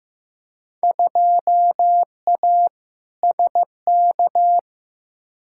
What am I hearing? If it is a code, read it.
Morse code, heard as a keyed tone, about 15 wpm, 700 Hz. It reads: 2A SK
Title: Field Day Exchanges Encore - 15wpm